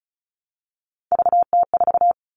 Morse code, heard as a keyed tone, about 34 wpm, 700 Hz. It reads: VT4
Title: All Cut Numbers - 3-Digit 35wpm